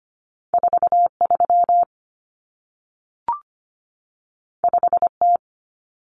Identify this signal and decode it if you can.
Morse code, heard as a keyed tone, about 25 wpm, 700 Hz. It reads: 43 5T
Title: Common Cut Numbers - 2-Digit 25wpm